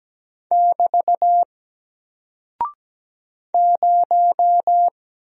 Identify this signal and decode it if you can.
Morse code, heard as a keyed tone, about 17 wpm, 700 Hz. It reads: <BT> 0
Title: Single Character (Letter:Number:Punctuation:Prosign) 17wpm